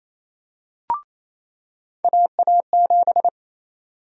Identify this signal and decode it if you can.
Morse code, heard as a keyed tone, about 28 wpm, 700 Hz. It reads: AA7
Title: USA Call Sign Prefixes - 28wpm